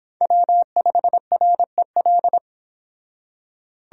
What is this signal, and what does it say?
Morse code, heard as a keyed tone, about 25 wpm, 700 Hz. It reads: W5REL
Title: Speed Racing - USA 1x3 Call Signs 22wpm